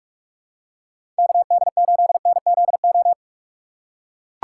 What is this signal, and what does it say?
Morse code, heard as a keyed tone, about 44 wpm, 700 Hz. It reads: KD8NZO
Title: USA Call Signs - 6 Characters - 6x Character Spacing - 45wpm